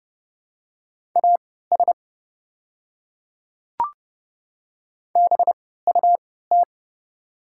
Morse code, heard as a keyed tone, about 30 wpm, spaced as for 17 wpm, 700 Hz. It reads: AS BUT
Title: 3x Character Spacing - Top 100 Words - 30wpm